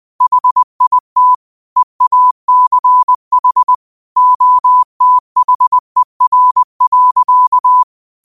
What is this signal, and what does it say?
Morse code, heard as a keyed tone, about 20 wpm, 1000 Hz. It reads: HIT EACH OTHER.
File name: qotdfiles / 20wpm